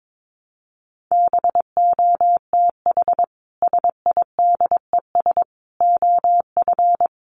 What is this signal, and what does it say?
Morse code, heard as a keyed tone, about 22 wpm, 700 Hz. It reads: BOTH SIDES OF
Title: Sets of 4 Words 22wpm